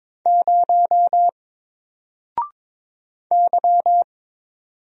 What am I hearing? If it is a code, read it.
Morse code, heard as a keyed tone, about 22 wpm, 700 Hz. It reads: 0 Y
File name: Single-Letter-Number-22wpm